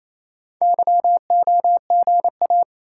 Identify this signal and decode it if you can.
Morse code, heard as a keyed tone, about 27 wpm, 700 Hz. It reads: YOGA
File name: Easy Words - Speed-Racing with Hints - 22wpm